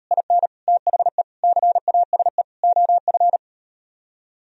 Morse code, heard as a keyed tone, about 38 wpm, 700 Hz. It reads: IN THE CASE OF